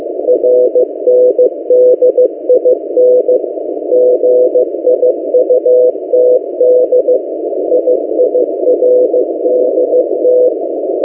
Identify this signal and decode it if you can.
Morse code, heard as a keyed tone, about 15 wpm, 500 Hz. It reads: RNDIN GIUTD IIRDT
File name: M08a- -1355utc-8096khz